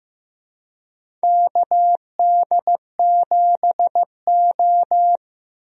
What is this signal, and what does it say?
Morse code, heard as a keyed tone, about 15 wpm, 700 Hz. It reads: KD7O
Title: USA Call Signs - 4 Characters - Repeated 3x - 15wpm